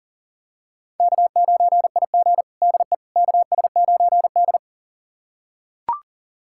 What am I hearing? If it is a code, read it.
Morse code, heard as a keyed tone, about 40 wpm, 700 Hz. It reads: K9IG DE KS9D
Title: USA Callsign DE Callsign - 4 Characters - 40wpm